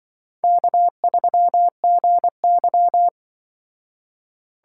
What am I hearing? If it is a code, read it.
Morse code, heard as a keyed tone, about 24 wpm, 700 Hz. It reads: K3GY
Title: Speed Racing - USA Call Signs 20wpm